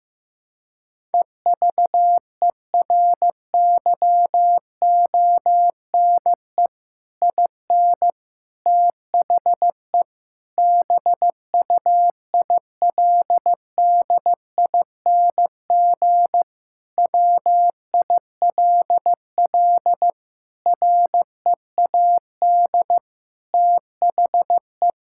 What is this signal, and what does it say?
Morse code, heard as a keyed tone, about 15 wpm, 700 Hz. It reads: EVERYONE IN THE BUILDING WILL READ THE
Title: Sentences from Top 1000 Words - Review Part 6 - 15wpm